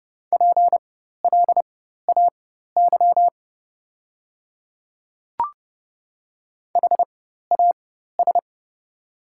Morse code, heard as a keyed tone, about 30 wpm, spaced as for 14 wpm, 700 Hz. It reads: PLAY HAS
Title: 4x Character Spacing - Top 200 Words - 30wpm